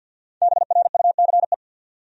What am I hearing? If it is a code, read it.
Morse code, heard as a keyed tone, about 47 wpm, 700 Hz. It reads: BRUCE
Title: USA Common Names - Max 5 Letters - 50wpm